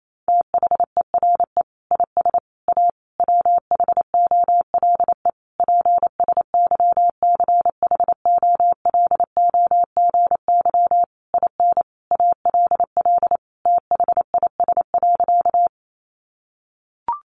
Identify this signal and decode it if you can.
Morse code, heard as a keyed tone, about 28 wpm, 700 Hz. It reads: THERE IS A WHOLE PSYCHOLOGY IN ALL THIS.
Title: Sentences from Top 1400 Words 28wpm